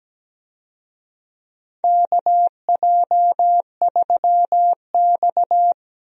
Morse code, heard as a keyed tone, about 17 wpm, 700 Hz. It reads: KJ3X
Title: USA Call Signs - 4 Characters - Repeated 2x - 17wpm